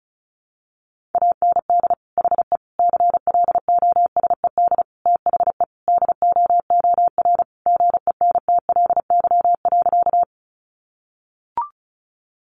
Morse code, heard as a keyed tone, about 34 wpm, 700 Hz. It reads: AND HE CLOSED THE DOOR GENTLY.
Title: Sentences from Top 1500 Words 35wpm